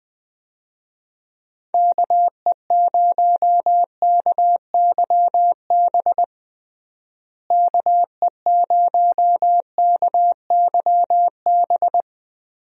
Morse code, heard as a keyed tone, about 20 wpm, 700 Hz. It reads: KE0KYB KE0KYB
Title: USA Call Signs - 6 Characters - Repeated 3x - 20wpm